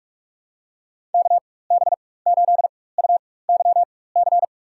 Morse code, heard as a keyed tone, about 46 wpm, spaced as for 21 wpm, 700 Hz. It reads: KB8UYC
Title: USA Call Signs - 4x Character Spacing - 45wpm